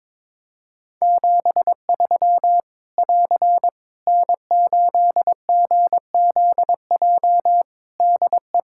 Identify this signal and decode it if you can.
Morse code, heard as a keyed tone, about 22 wpm, 700 Hz. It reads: 73 <AR> N8GZJ DE
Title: Practice QSO - Exchange 4 - Ending - 22wpm